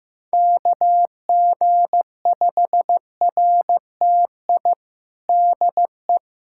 Morse code, heard as a keyed tone, about 15 wpm, 700 Hz. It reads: KG5RTI DE
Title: USA Callsign DE Callsign 15wpm